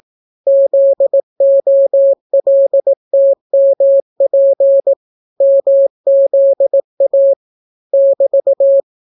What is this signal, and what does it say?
Morse code, heard as a keyed tone, about 18 wpm, 550 Hz. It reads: ZOLTMP MZA <BT>